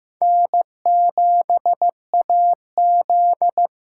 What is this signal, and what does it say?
Morse code, heard as a keyed tone, about 16 wpm, 700 Hz. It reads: N7AZ
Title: K1USN SST QSO Practice - 15wpm